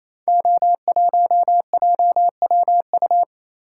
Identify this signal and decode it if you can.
Morse code, heard as a keyed tone, about 27 wpm, 700 Hz. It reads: O1JWU
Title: Code Groups - 5 Letter:Numbers - 28wpm